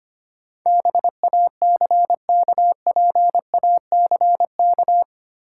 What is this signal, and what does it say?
Morse code, heard as a keyed tone, about 25 wpm, 700 Hz. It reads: BACKPACK
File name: practice / 25wpm / Easy Words with Hints - 25wpm